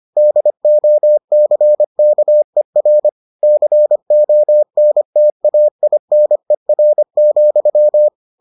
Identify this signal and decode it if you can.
Morse code, heard as a keyed tone, about 25 wpm, 600 Hz. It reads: DOCKER CONTAINER,